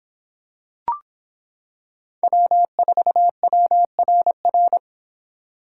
Morse code, heard as a keyed tone, about 25 wpm, 700 Hz. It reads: W4WRR